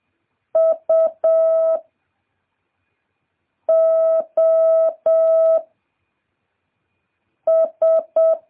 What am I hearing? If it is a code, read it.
Morse code, heard as a keyed tone, about 7 wpm, 650 Hz. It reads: U O S